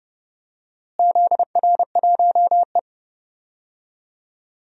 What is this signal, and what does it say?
Morse code, heard as a keyed tone, about 30 wpm, 700 Hz. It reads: ZR1E
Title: DXCC-100 Call Signs - 4 Characters - Repeated 2x - 30wpm